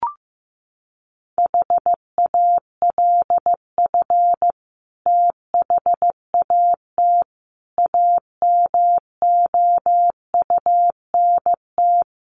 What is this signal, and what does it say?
Morse code, heard as a keyed tone, about 14 wpm, 700 Hz. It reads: HALF THAT AMOUNT
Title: Sentences from Top 1400 Words 15wpm